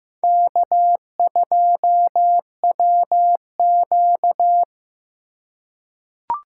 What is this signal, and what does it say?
Morse code, heard as a keyed tone, about 15 wpm, 700 Hz. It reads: K2WQ
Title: USA Call Signs - 4 Characters - 4x Character Spacing - 15wpm